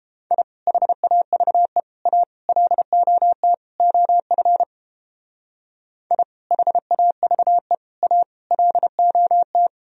Morse code, heard as a keyed tone, about 31 wpm, 700 Hz. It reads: I HAVE A LOT OF I HAVE A LOT